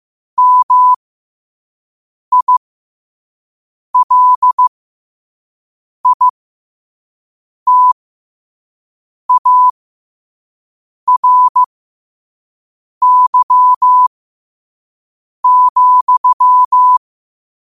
Morse code, heard as a keyed tone, about 15 wpm, spaced as for 5 wpm, 1000 Hz. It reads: MILITARY,
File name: qotdfiles / 5wpm